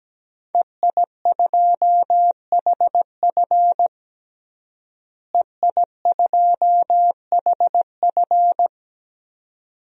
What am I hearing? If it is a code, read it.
Morse code, heard as a keyed tone, about 17 wpm, 700 Hz. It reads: EI2HF EI2HF
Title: DXCC-100 Call Signs - 5 Characters - Repeated 2x - 17wpm